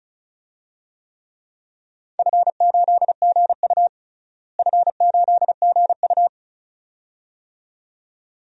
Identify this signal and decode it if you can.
Morse code, heard as a keyed tone, about 36 wpm, 700 Hz. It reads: F8GU F8GU
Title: DXCC-100 Call Signs - 4 Characters - Repeated 2x - 35wpm